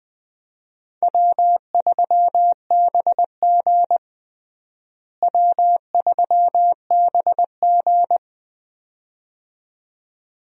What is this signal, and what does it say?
Morse code, heard as a keyed tone, about 20 wpm, 700 Hz. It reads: W3BG W3BG
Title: USA Call Signs - 4 Characters - Repeated 2x - 20wpm